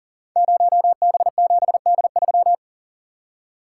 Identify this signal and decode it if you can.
Morse code, heard as a keyed tone, about 42 wpm, 690 Hz. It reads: 0B7D3
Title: Code Groups - 5 Letter:Numbers - 40wpm